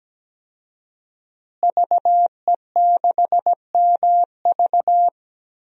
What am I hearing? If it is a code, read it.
Morse code, heard as a keyed tone, about 17 wpm, 700 Hz. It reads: VE6MV
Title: DXCC-100 Call Signs - 5 Characters - Repeated 3x - 17wpm